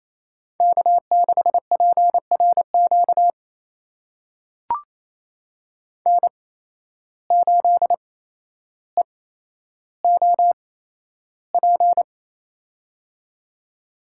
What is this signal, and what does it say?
Morse code, heard as a keyed tone, about 28 wpm, spaced as for 12 wpm, 700 Hz. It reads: K6PRQ N8EOP
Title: USA Call Signs - 5 Characters - 8x Character Spacing - 28wpm